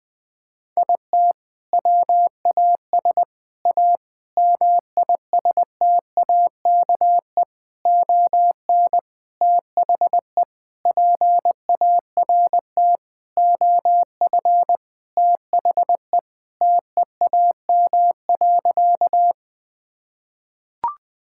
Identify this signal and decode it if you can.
Morse code, heard as a keyed tone, about 20 wpm, 700 Hz. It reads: IT WAS A MISTAKE ON THE PART OF THE TEAM.
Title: Sentences with 4-Word Phrases - 20wpm